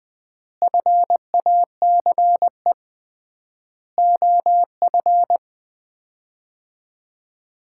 Morse code, heard as a keyed tone, about 20 wpm, 700 Hz. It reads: FACE OF
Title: 3x Word-Spacing - Sets of 4 Words - 20wpm